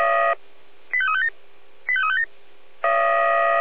Five-tone selective call (ZVEI), heard as quick runs of five tones, tones of about 70 ms. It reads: 85357 85357